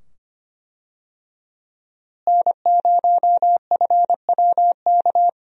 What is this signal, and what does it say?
Morse code, heard as a keyed tone, about 25 wpm, 700 Hz. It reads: N0FWK
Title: USA Call Signs - 5 Characters - Repeated 2x - 25wpm